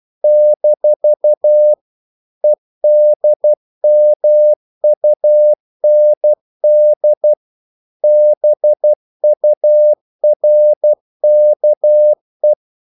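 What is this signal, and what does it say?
Morse code, heard as a keyed tone, about 12 wpm, 600 Hz. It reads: - EDMUND BURKE
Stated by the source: Morse practice sending English quotations.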